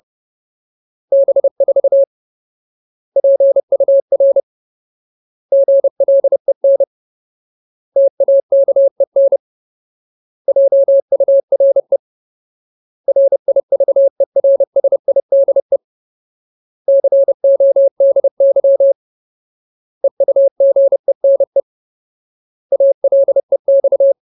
Morse code, heard as a keyed tone, about 30 wpm, 550 Hz. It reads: B4 PUR GLEN TAKEN JURE RIVERSIDE CODY EUGENE ALEX
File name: Ultimate Word List / 30WPM Ultimate